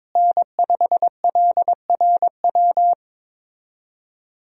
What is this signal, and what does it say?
Morse code, heard as a keyed tone, about 23 wpm, 700 Hz. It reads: N5LRW
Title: USA Call Signs - 5 Characters - Repeated 2x - 22wpm